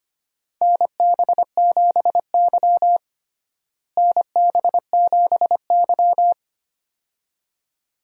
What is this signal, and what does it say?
Morse code, heard as a keyed tone, about 25 wpm, 700 Hz. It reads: NB7Y NB7Y
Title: USA Call Signs - 4 Characters - Repeated 2x - 25wpm